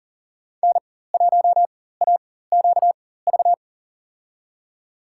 Morse code, heard as a keyed tone, about 38 wpm, spaced as for 18 wpm, 700 Hz. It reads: N1AQV
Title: USA Call Signs - 5 Characters - 4x Character Spacing - 40wpm